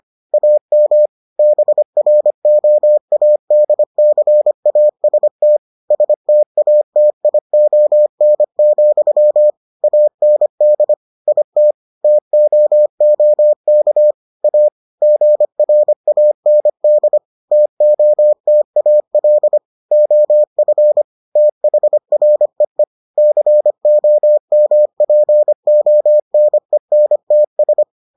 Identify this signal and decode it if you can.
Morse code, heard as a keyed tone, about 25 wpm, 600 Hz. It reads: AM BROADCAST STATION, AND IT TOOK A GRAND TOTAL OF THREE COMPONENTS